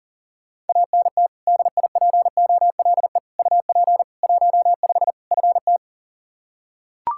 Morse code, heard as a keyed tone, about 37 wpm, 700 Hz. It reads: ANT DIPOLE UP 15 FT
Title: Practice QSO - Exchange 2 - Info - 40wpm